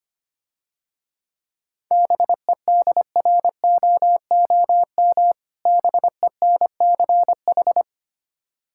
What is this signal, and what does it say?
Morse code, heard as a keyed tone, about 25 wpm, 700 Hz. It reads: BEDROOM BENCH
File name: practice / 25wpm / Difficult Words - Speed-Racing with Hints - 25wpm